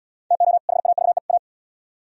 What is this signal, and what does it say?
Morse code, heard as a keyed tone, about 69 wpm, 700 Hz. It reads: TO HAVE A